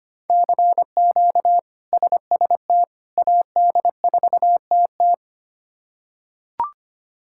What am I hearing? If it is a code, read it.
Morse code, heard as a keyed tone, about 25 wpm, 700 Hz. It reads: CQ SST AD4TT
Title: K1USN SST QSO Practice - 25wpm